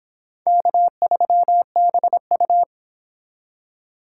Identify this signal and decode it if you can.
Morse code, heard as a keyed tone, about 22 wpm, 700 Hz. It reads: K3BU